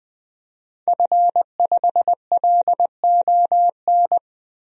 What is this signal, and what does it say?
Morse code, heard as a keyed tone, about 20 wpm, 700 Hz. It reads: F5LON
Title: DXCC-100 Call Signs - 5 Characters - Repeated 2x - 20wpm